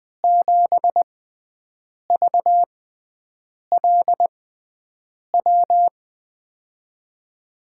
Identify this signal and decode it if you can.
Morse code, heard as a keyed tone, about 20 wpm, spaced as for 7 wpm, 700 Hz. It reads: 7VLW
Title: USA Call Signs - 6 Characters - 6x Character Spacing - 20wpm